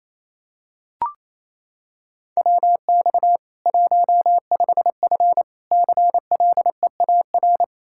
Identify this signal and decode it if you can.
Morse code, heard as a keyed tone, about 28 wpm, 700 Hz. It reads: WX 15F CLEAR
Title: Practice QSO - Exchange 2 - Info - 28wpm